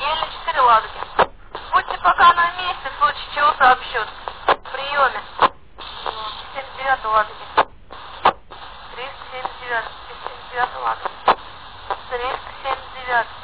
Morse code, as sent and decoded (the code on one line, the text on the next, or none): none